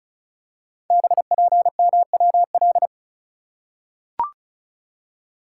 Morse code, heard as a keyed tone, about 33 wpm, 690 Hz. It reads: BPMWL
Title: Code Groups - 5 Letters - 35wpm